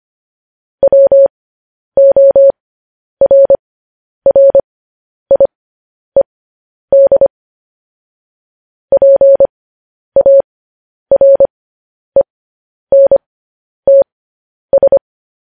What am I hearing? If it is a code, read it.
Morse code, heard as a keyed tone, about 25 wpm, spaced as for 10 wpm, 550 Hz. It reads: WORRIED PARENTS
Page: news-headlines-in-morse-code-at-10-wpm